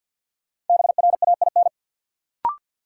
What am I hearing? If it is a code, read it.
Morse code, heard as a keyed tone, about 53 wpm, 700 Hz. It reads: BRAIN
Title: Sentences from Top 1100 Words - New Word Drill Alt - 50wpm